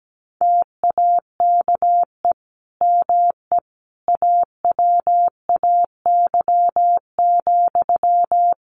Morse code, heard as a keyed tone, about 17 wpm, 700 Hz. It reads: TAKE ME AWAY,
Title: Sentences from Top 1400 Words 17wpm